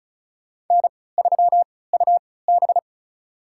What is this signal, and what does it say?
Morse code, heard as a keyed tone, about 34 wpm, spaced as for 20 wpm, 700 Hz. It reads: N3UB
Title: USA Call Signs - 4 Characters - 3x Character Spacing - 35wpm